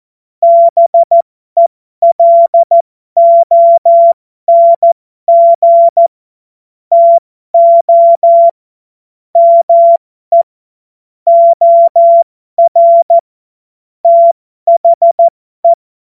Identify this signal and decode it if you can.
Morse code, heard as a keyed tone, about 14 wpm, spaced as for 12 wpm, 680 Hz. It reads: BELONG TO ME OR THE